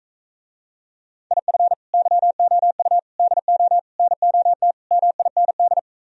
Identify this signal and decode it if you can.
Morse code, heard as a keyed tone, about 45 wpm, 700 Hz. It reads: IF YOU DO NOT MIND